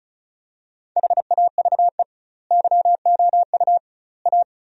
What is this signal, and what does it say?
Morse code, heard as a keyed tone, about 34 wpm, 700 Hz. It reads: HAVE YOU A
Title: 2x Word-Spacing - Sentences from Top 400 Words - 35wpm